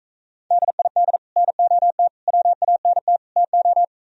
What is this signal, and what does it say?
Morse code, heard as a keyed tone, about 45 wpm, 700 Hz. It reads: DID NOT WANT TO